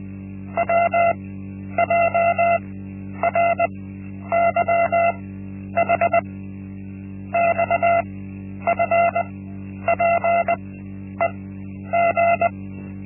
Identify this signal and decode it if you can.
Morse code, heard as a keyed tone, about 20 wpm, spaced as for 10 wpm, 680 Hz. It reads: WJRYH XFPEG
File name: Linksmakalnis-1